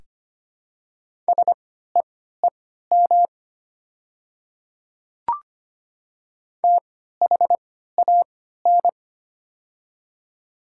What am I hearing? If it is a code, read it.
Morse code, heard as a keyed tone, about 25 wpm, spaced as for 14 wpm, 700 Hz. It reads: SEEM THAN
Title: 3x Character Spacing - Top 500 Words - 25wpm